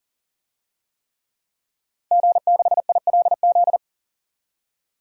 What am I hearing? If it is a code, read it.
Morse code, heard as a keyed tone, about 40 wpm, 700 Hz. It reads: G6ILZ
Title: DXCC-100 Call Signs - 5 Characters - 40wpm